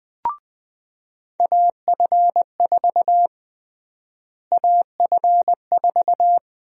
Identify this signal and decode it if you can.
Morse code, heard as a keyed tone, about 21 wpm, 700 Hz. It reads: AF4 AF4
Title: USA Call Sign Prefixes - Repeated 2x - 20wpm